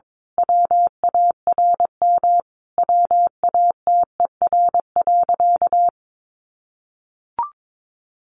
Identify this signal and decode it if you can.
Morse code, heard as a keyed tone, about 22 wpm, 700 Hz. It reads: WARM WATER.
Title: A Sound Of Thunder 22wpm